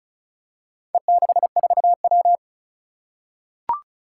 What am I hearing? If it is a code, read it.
Morse code, heard as a keyed tone, about 36 wpm, 700 Hz. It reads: E64W
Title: Four Challenging Letter-Number Combinations 35wpm